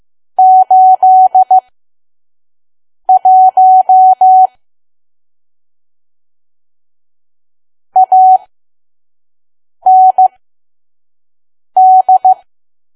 Morse code, heard as a keyed tone, about 15 wpm, spaced as for 5 wpm, 750 Hz. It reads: 81 AND
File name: Archive / 5 WPM files / PM